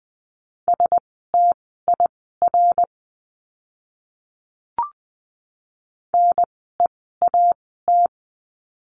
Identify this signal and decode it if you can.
Morse code, heard as a keyed tone, about 20 wpm, spaced as for 15 wpm, 700 Hz. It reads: STIR NEAT
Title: 054 Lesson 10f - Letter r - Words 20wpm